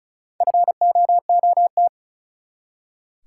Morse code, heard as a keyed tone, about 36 wpm, 700 Hz. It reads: FOOT
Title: Four Letter Words - 35wpm